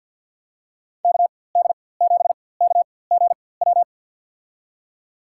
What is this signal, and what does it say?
Morse code, heard as a keyed tone, about 53 wpm, spaced as for 24 wpm, 700 Hz. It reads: KD7KGW
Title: USA Call Signs - 6 Characters - 4x Character Spacing - 50wpm